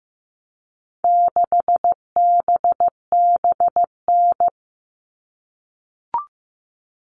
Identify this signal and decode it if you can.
Morse code, heard as a keyed tone, about 15 wpm, 700 Hz. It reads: 6BBN
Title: All Cut Numbers - 4-Digit 15wpm